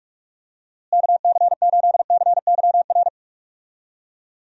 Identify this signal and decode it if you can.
Morse code, heard as a keyed tone, about 44 wpm, 700 Hz. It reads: KC8CYR
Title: USA Call Signs - 6 Characters - Repeated 3x - 45wpm